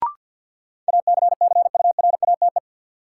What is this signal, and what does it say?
Morse code, heard as a keyed tone, about 52 wpm, 700 Hz. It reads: ACCURATE
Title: Sentences from Top 1200 Words - New Word Drill - 50wpm